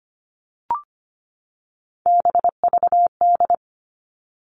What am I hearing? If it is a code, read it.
Morse code, heard as a keyed tone, about 25 wpm, 700 Hz. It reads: BVD
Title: Three Challenging Letter-Number Combinations 25wpm